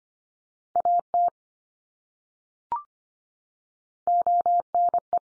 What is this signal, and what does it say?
Morse code, heard as a keyed tone, about 25 wpm, 700 Hz. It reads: AT ONE
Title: Top 100 Words 25wpm x3